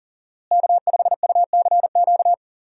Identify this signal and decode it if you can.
Morse code, heard as a keyed tone, about 40 wpm, 700 Hz. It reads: K5UCQ
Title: USA Call Signs - 5 Characters - Repeated 3x - 40wpm